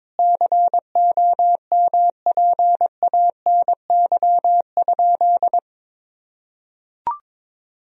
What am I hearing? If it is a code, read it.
Morse code, heard as a keyed tone, about 22 wpm, 700 Hz. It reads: COMPANY?
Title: Sentences from Top 1000 Words - Review Part 3 - 22wpm